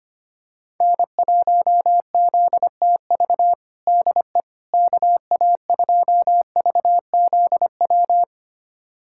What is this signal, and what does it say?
Morse code, heard as a keyed tone, about 25 wpm, 700 Hz. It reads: N1ZTV DE KA2VZW
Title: USA Callsign DE Callsign 25wpm